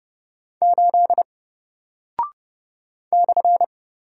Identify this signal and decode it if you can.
Morse code, heard as a keyed tone, about 30 wpm, 700 Hz. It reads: 8 /
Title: Single Character (Letter:Number:Punctuation:Prosign) Alt 30wpm